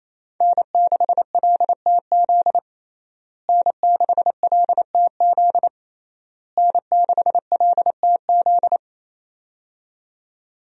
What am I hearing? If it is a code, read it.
Morse code, heard as a keyed tone, about 28 wpm, 700 Hz. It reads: N6LTZ N6LTZ N6LTZ
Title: USA Call Signs - 5 Characters - Repeated 3x - 28wpm